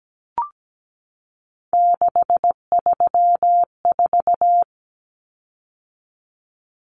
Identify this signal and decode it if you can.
Morse code, heard as a keyed tone, about 17 wpm, 700 Hz. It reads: 634
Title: Common Cut Numbers - 3-Digit 17wpm